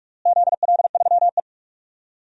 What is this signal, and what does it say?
Morse code, heard as a keyed tone, about 47 wpm, 700 Hz. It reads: ZL3E
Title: DXCC-100 Call Signs - 4 Characters - Repeated 3x - 45wpm